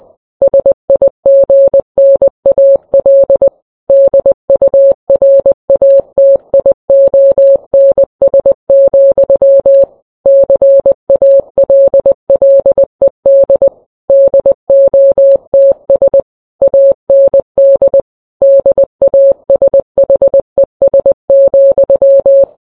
Morse code, heard as a keyed tone, about 20 wpm, 550 Hz. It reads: SIGNAL DURATIONS, CALLED DOTS AND DASHES,